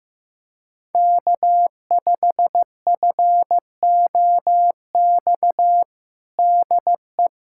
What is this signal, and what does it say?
Morse code, heard as a keyed tone, about 15 wpm, 700 Hz. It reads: K5FOX DE
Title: USA Callsign DE Callsign - 5 Characters - 15wpm